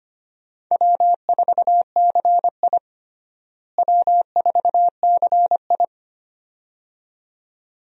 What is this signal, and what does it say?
Morse code, heard as a keyed tone, about 25 wpm, 700 Hz. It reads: W4CI W4CI
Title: USA Call Signs - 4 Characters - Repeated 2x - 25wpm